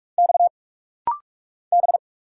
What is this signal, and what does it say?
Morse code, heard as a keyed tone, about 46 wpm, 700 Hz. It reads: X B
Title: Single Letter-Number Alt - Rapid-Fire 45wpm